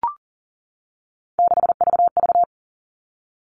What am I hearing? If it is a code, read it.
Morse code, heard as a keyed tone, about 40 wpm, 700 Hz. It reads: 6VV
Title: All Cut Numbers - 3-Digit 40wpm